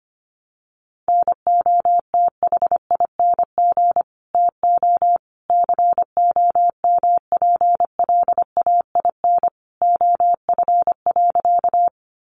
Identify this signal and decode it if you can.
Morse code, heard as a keyed tone, about 25 wpm, 700 Hz. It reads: NOTHING TO COMPLAIN OF.
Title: Sentences from Top 1300 Words 25wpm